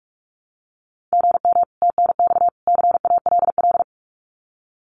Morse code, heard as a keyed tone, about 44 wpm, 700 Hz. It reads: GM TNX CALL
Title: Common QSO Phrases 45wpm